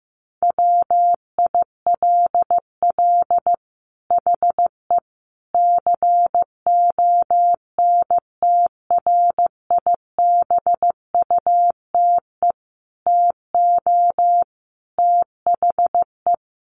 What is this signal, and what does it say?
Morse code, heard as a keyed tone, about 15 wpm, 700 Hz. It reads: WILL HE CONTRIBUTE TO THE
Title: Sentences from Top 1400 Words 15wpm